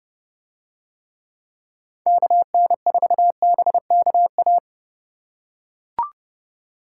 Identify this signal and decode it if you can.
Morse code, heard as a keyed tone, about 30 wpm, 700 Hz. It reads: KN4BKA